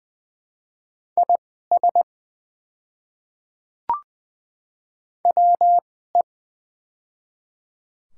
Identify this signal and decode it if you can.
Morse code, heard as a keyed tone, about 20 wpm, spaced as for 14 wpm, 700 Hz. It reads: IS WE
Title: 2x Character Spacing - Top 100 Words - 20wpm